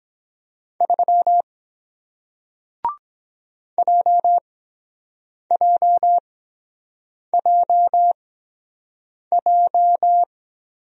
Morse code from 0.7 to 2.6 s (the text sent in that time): 3